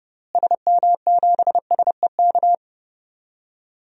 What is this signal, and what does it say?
Morse code, heard as a keyed tone, about 30 wpm, 700 Hz. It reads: SM7SEK
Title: DXCC-100 Call Signs - 6 Characters - Repeated 3x - 30wpm